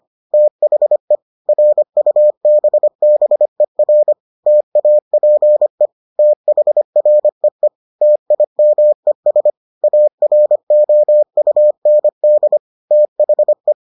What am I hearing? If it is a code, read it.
Morse code, heard as a keyed tone, about 25 wpm, 600 Hz. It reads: THE RUBBER TAPE THREE TIMES AROUND THE